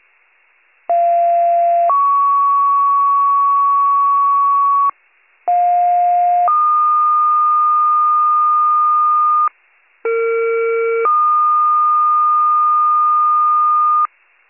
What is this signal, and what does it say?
Two-tone page, heard as a one-second tone, then a three-second tone, about 690 then 1120 Hz, about 710 then 1220 Hz, about 460 then 1220 Hz. It is heard from 0.9 to 14.1 s.